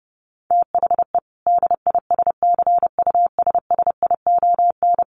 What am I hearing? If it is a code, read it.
Morse code, heard as a keyed tone, about 30 wpm, 700 Hz. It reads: THE DISCUSSION
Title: Sentences from Top 1100 Words 30wpm